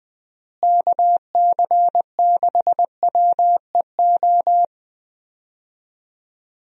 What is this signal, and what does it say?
Morse code, heard as a keyed tone, about 21 wpm, 700 Hz. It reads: KC6WEO
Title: USA Call Signs - 6 Characters - Repeated 2x - 20wpm